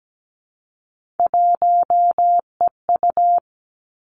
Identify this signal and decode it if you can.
Morse code, heard as a keyed tone, about 17 wpm, 700 Hz. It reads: 1EU
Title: Three Challenging Letter-Number Combinations 17wpm